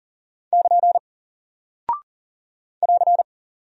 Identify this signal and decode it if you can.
Morse code, heard as a keyed tone, about 40 wpm, 700 Hz. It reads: <KN> <AR>
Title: Single Character (Letter:Number:Punctuation:Prosign) Alt 40wpm